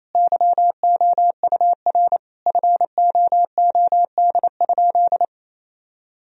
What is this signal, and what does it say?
Morse code, heard as a keyed tone, about 28 wpm, 700 Hz. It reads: YOUR FOOD?
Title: Sentences from Top 1000 Words - Review Part 2 - 28wpm